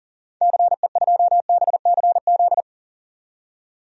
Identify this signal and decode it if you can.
Morse code, heard as a keyed tone, about 38 wpm, 700 Hz. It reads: CE2BCZ
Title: DXCC-100 Call Signs - 6 Characters - 40wpm